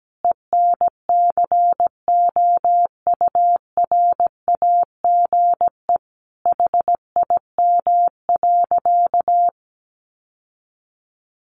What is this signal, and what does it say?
Morse code, heard as a keyed tone, about 17 wpm, 700 Hz. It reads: ENCOURAGE HIM.
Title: Sentences from Top 1400 Words 17wpm